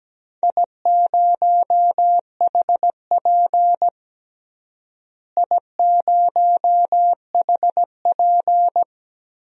Morse code, heard as a keyed tone, about 17 wpm, 700 Hz. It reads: I0HP I0HP
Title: DXCC-100 Call Signs - 4 Characters - Repeated 3x - 17wpm